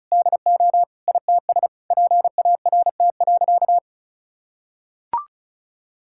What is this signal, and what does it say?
Morse code, heard as a keyed tone, about 36 wpm, 700 Hz. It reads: DO ITS PART.
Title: Sentences from Top 1000 Words - Review Part 5 - 35wpm